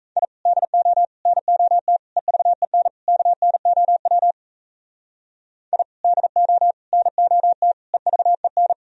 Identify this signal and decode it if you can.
Morse code, heard as a keyed tone, about 34 wpm, 700 Hz. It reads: I DO NOT EVEN KNOW I DO NOT EVEN
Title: Speed-Racing - Sets of 5 Words 28wpm